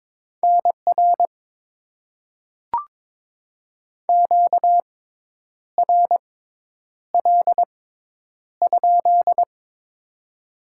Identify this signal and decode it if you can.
Morse code, heard as a keyed tone, about 22 wpm, spaced as for 9 wpm, 700 Hz. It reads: NR QRL?
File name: Top 90 QSO Elements - 6x Character Spacing - 22wpm